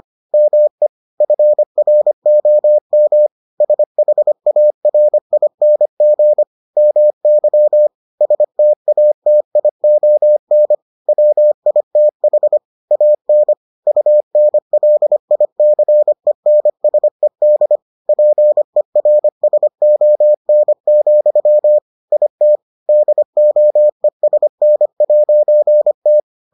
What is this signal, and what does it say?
Morse code, heard as a keyed tone, about 25 wpm, 600 Hz. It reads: ME FROM SHARING MY STATION WITH AN UNLICENSED PERSON, IT DOESN'T